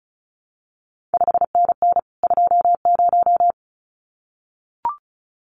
Morse code, heard as a keyed tone, about 36 wpm, 700 Hz. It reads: 5NN 20
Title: CQ WW DX Contest Exchange - 35wpm